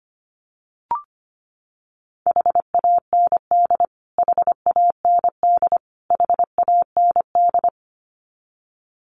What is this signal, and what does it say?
Morse code, heard as a keyed tone, about 25 wpm, 700 Hz. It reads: HAND HAND HAND
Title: Top 200 Words Repeated 25wpm